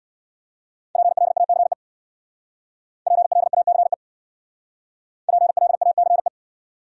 Morse code, heard as a keyed tone, about 64 wpm, 700 Hz. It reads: PLACE PLACE PLACE